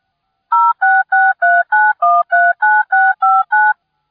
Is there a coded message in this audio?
Yes: DTMF from 0.5 to 3.7 s, heard as telephone touch tones, tones of about 220 ms. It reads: 06639139659